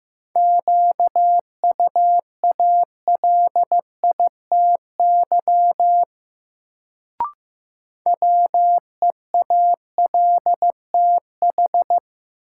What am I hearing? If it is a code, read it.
Morse code, heard as a keyed tone, about 15 wpm, 700 Hz. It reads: QUALITY WEALTH
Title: Sentences from Top 1000 Words - Review Part 4 - New Word Drill - 15wpm